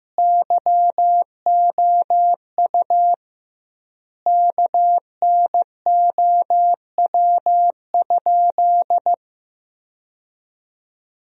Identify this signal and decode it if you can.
Morse code, heard as a keyed tone, about 15 wpm, 700 Hz. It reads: YOU KNOW?
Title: 2x Word-Spacing - Sentences from Top 100 Words - 15wpm